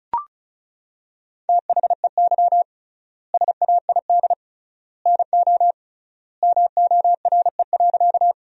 Morse code, heard as a keyed tone, about 36 wpm, 700 Hz. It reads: THEY SAID NO MORE.
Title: 3x Word-Spacing - Sentences from Top 100 Words - 35wpm